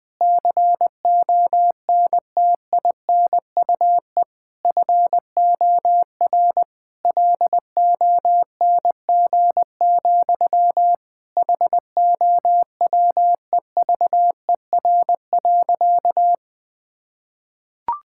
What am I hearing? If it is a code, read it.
Morse code, heard as a keyed tone, about 21 wpm, 700 Hz. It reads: CONTINUE FOR LONG, HOWEVER.
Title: Sentences from Top 1000 Words - Review Part 6 - 20wpm